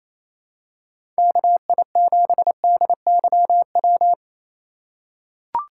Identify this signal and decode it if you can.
Morse code, heard as a keyed tone, about 28 wpm, 700 Hz. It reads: KI7DYW
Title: USA Call Signs - 6 Characters - 3x Character Spacing - 28wpm